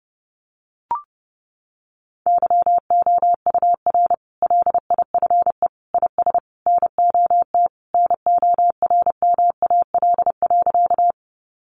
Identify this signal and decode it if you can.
Morse code, heard as a keyed tone, about 30 wpm, 700 Hz. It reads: YOUR LIFE IS NOT NORMAL.
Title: Sentences from Top 1200 Words 30wpm